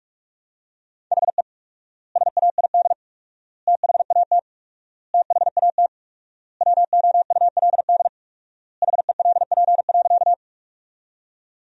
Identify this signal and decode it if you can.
Morse code, heard as a keyed tone, about 46 wpm, 700 Hz. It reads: HE SAID THAT THAT WOULD HELP.
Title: 4x Word-Spacing - Sentences from Top 200 Words - 45wpm